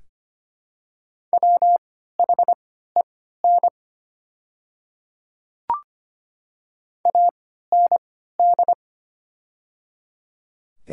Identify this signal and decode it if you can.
Morse code, heard as a keyed tone, about 25 wpm, spaced as for 14 wpm, 700 Hz. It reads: WHEN AND
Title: 3x Character Spacing - Top 200 Words - 25wpm